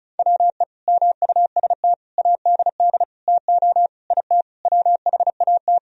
Morse code, heard as a keyed tone, about 36 wpm, 700 Hz. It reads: WE MUST ADD TO IT WHAT
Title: Sentences from Top 1000 Words - Review Part 1 - 35wpm